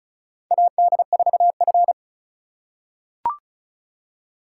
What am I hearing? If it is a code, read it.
Morse code, heard as a keyed tone, about 36 wpm, 700 Hz. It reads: AD4F
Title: USA Call Signs - 2x Character Spacing - 35wpm